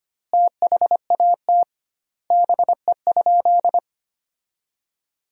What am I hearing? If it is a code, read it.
Morse code, heard as a keyed tone, about 25 wpm, 700 Hz. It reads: THAT BE?
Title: 2x Word-Spacing - Sentences from Top 100 Words - 25wpm